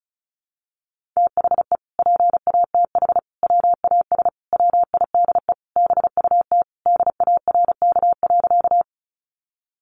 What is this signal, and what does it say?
Morse code, heard as a keyed tone, about 34 wpm, 700 Hz. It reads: THE PATH WAS WIDE BUT DARK.
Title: Sentences from Top 1400 Words 35wpm